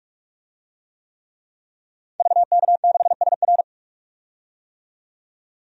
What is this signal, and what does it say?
Morse code, heard as a keyed tone, about 44 wpm, 700 Hz. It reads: VK6SR
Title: DXCC-100 Call Signs - 5 Characters - 45wpm